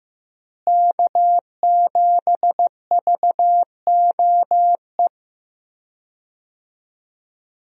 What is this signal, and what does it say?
Morse code, heard as a keyed tone, about 15 wpm, 700 Hz. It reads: K7VOE
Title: USA Call Signs - 5 Characters - Repeated 2x - 15wpm